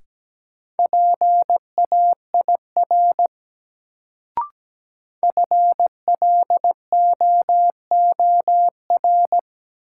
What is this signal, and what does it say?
Morse code, heard as a keyed tone, about 17 wpm, 700 Hz. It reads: PAIR FLOOR
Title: Sets of 3 Words - New Word Drill - 17wpm